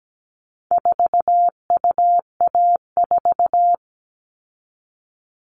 Morse code, heard as a keyed tone, about 17 wpm, 700 Hz. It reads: 4UA4
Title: All Cut Numbers - 4-Digit 17wpm